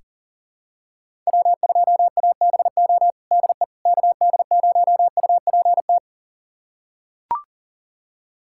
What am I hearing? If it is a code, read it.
Morse code, heard as a keyed tone, about 37 wpm, 700 Hz. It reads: W2ABO DE KD0UPT
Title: USA Callsign DE Callsign 40wpm